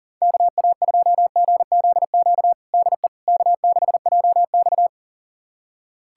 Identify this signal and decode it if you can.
Morse code, heard as a keyed tone, about 40 wpm, 700 Hz. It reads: KA2GZQ DE K6JX
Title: USA Callsign DE Callsign 40wpm